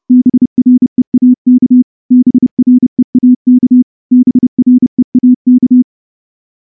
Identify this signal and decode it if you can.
Morse code, heard as a keyed tone, about 29 wpm, 250 Hz. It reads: BREAK BREAK BREAK
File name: morse-training-30WPM